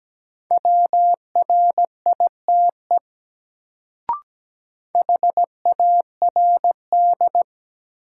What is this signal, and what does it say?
Morse code, heard as a keyed tone, about 17 wpm, 700 Hz. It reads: WRITE HARD
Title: Sentences from Top 300 Words - New Word Drill - 17wpm